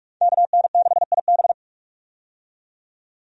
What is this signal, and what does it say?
Morse code, heard as a keyed tone, about 46 wpm, 700 Hz. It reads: KN6IB
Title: USA Call Signs - 5 Characters - 45wpm